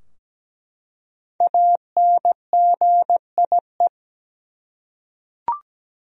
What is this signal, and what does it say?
Morse code, heard as a keyed tone, about 17 wpm, 700 Hz. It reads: ANGIE
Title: USA Common Names - Max 5 Letters - 17wpm